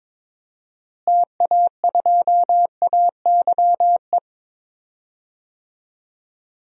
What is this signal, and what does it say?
Morse code, heard as a keyed tone, about 22 wpm, 700 Hz. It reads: TA2AYE
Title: DXCC-100 Call Signs - 6 Characters - Repeated 2x - 22wpm